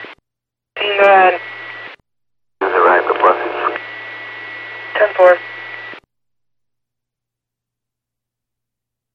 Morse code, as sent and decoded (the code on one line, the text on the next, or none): none